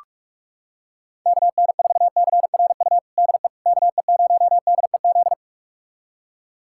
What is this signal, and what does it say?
Morse code, heard as a keyed tone, about 47 wpm, 700 Hz. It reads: KN4CRU DE KE0DEZ
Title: USA Callsign DE Callsign - 6 Characters - 45wpm